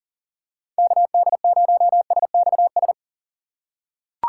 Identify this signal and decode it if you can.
Morse code, heard as a keyed tone, about 37 wpm, 700 Hz. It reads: KD0SXS
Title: USA Call Signs - 6 Characters - 3x Character Spacing - 40wpm